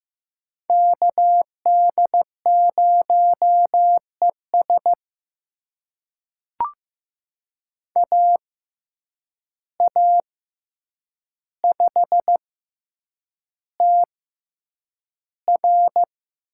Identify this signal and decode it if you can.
Morse code, heard as a keyed tone, about 15 wpm, spaced as for 8 wpm, 700 Hz. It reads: KD0ES AA5TR
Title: USA Call Signs - 5 Characters - 6x Character Spacing - 15wpm